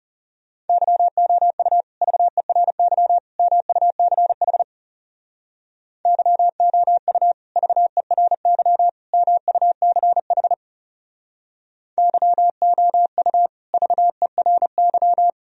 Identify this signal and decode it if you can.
Morse code, heard as a keyed tone, about 36 wpm, 700 Hz. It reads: YOU VERY MUCH YOU VERY MUCH YOU VERY
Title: Speed-Racing - Sets of 3 Words 30wpm